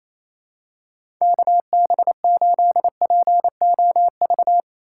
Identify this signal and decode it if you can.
Morse code, heard as a keyed tone, about 27 wpm, 700 Hz. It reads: KB8POV
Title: USA Call Signs - 3x Character Spacing - 28wpm